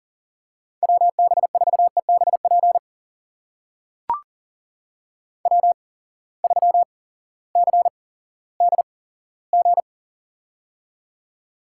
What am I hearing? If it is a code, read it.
Morse code, heard as a keyed tone, about 40 wpm, spaced as for 18 wpm, 700 Hz. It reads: WB4EBP W3CDG
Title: USA Call Signs - 8x Character Spacing - 40wpm